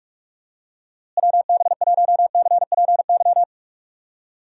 Morse code, heard as a keyed tone, about 46 wpm, 700 Hz. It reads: WB1CPY
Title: USA Call Signs - 6 Characters - Repeated 2x - 45wpm